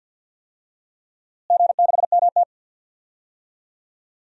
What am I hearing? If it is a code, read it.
Morse code, heard as a keyed tone, about 46 wpm, 700 Hz. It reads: G6MT
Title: DXCC-100 Call Signs - 4 Characters - Repeated 3x - 50wpm